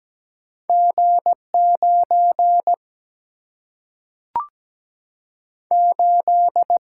Morse code, heard as a keyed tone, about 17 wpm, 700 Hz. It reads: G9 8
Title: Single-Letter Single-Number 17wpm